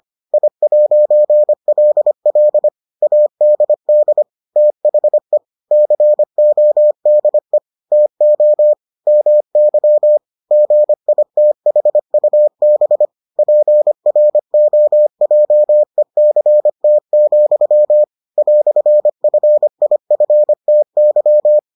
Morse code, heard as a keyed tone, about 25 wpm, 600 Hz. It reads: I'LL ADD THE CODE TO MY GITHUB PROJECT, "FIFTY